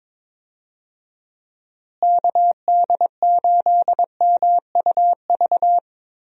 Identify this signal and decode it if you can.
Morse code, heard as a keyed tone, about 23 wpm, 700 Hz. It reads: KD8MUV
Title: USA Call Signs - 6 Characters - Repeated 3x - 22wpm